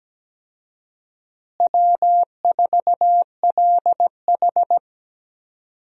Morse code, heard as a keyed tone, about 17 wpm, 700 Hz. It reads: W4LH
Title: USA Call Signs - 4 Characters - Repeated 2x - 17wpm